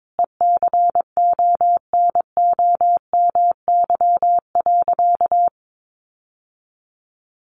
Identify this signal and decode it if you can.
Morse code, heard as a keyed tone, about 22 wpm, 700 Hz. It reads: ECONOMY.
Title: Sentences from Top 1200 Words 22wpm